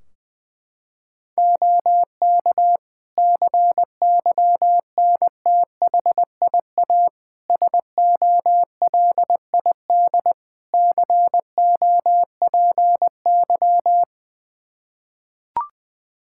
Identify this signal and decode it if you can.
Morse code, heard as a keyed tone, about 20 wpm, 700 Hz. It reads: OK CYNTHIA SOLID COPY
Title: Practice QSO - Exchange 3 - More Info - 20wpm